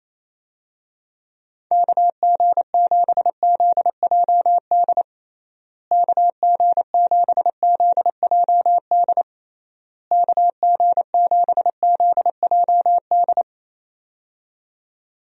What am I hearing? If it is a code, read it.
Morse code, heard as a keyed tone, about 28 wpm, 700 Hz. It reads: KG7ZJD KG7ZJD KG7ZJD
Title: USA Call Signs - 6 Characters - Repeated 3x - 28wpm